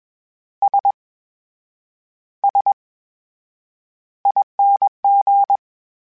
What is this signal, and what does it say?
Morse code, heard as a keyed tone, about 21 wpm, spaced as for 8 wpm, 800 Hz. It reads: SSING
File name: Special INGs-mp3s / Special INGs-mp3s / VerySlowToFaster Ing @20